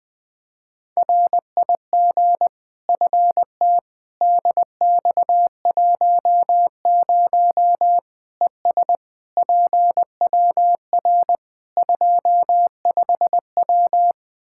Morse code, heard as a keyed tone, about 20 wpm, 700 Hz. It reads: RIG FT DX10 ES PWR 25W
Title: Practice QSO - Exchange 2 - Info - 20wpm